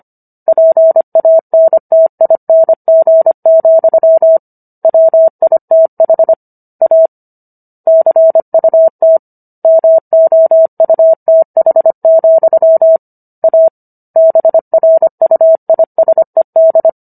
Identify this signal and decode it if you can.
Morse code, heard as a keyed tone, about 25 wpm, 650 Hz. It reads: PANTING, WITH A CUT MOUTH, A BRUISED